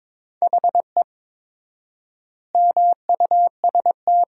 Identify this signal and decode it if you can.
Morse code, heard as a keyed tone, about 22 wpm, 700 Hz. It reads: HE MUST
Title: 4x Word-Spacing - Sentences from Top 500 Words - 22wpm